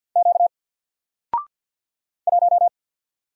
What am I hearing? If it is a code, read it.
Morse code, heard as a keyed tone, about 48 wpm, 700 Hz. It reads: Q 1
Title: Single Letter-Number Alt 50wpm